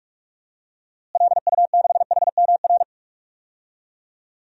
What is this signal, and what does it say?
Morse code, heard as a keyed tone, about 46 wpm, 700 Hz. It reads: LU6HMR
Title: DXCC-100 Call Signs - 6 Characters - Repeated 3x - 45wpm